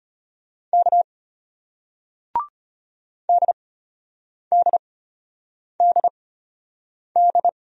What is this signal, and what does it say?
Morse code, heard as a keyed tone, about 33 wpm, spaced as for 8 wpm, 700 Hz. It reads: K DDDD